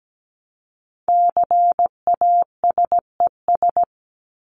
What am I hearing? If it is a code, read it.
Morse code, heard as a keyed tone, about 17 wpm, 700 Hz. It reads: CASES
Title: ES:S Suffix - Top 100 Words with Base Words 17wpm